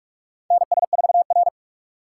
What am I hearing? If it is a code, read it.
Morse code, heard as a keyed tone, about 47 wpm, 700 Hz. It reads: NS4R
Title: USA Call Signs - 4 Characters - 2x Character Spacing - 45wpm